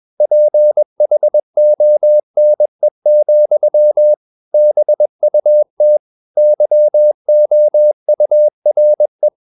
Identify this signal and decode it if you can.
Morse code, heard as a keyed tone, about 21 wpm, 600 Hz. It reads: PHONE, BUT YOURE
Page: CW med Gnister